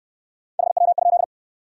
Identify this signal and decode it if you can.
Morse code, heard as a keyed tone, about 63 wpm, 700 Hz. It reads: HW?